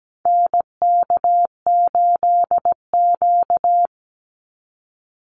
Morse code, heard as a keyed tone, about 17 wpm, 700 Hz. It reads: NK8Q
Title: NA SOTA Activator Callsigns 17wpm